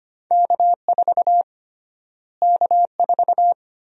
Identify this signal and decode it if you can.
Morse code, heard as a keyed tone, about 25 wpm, 700 Hz. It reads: K4 K4
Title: USA Call Sign Prefixes - Repeated 2x - 25wpm